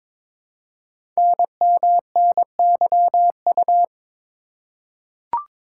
Morse code, heard as a keyed tone, about 22 wpm, 700 Hz. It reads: NMNYU
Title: Code Groups - 5 Letters - 22wpm